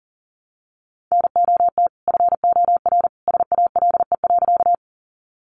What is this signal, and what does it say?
Morse code, heard as a keyed tone, about 40 wpm, 700 Hz. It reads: NOT FOR SALE.
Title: Sentences from Top 1400 Words 40wpm